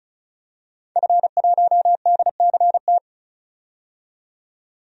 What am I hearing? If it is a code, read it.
Morse code, heard as a keyed tone, about 35 wpm, 700 Hz. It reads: F1DCT
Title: DXCC-100 Call Signs - 5 Characters - 35wpm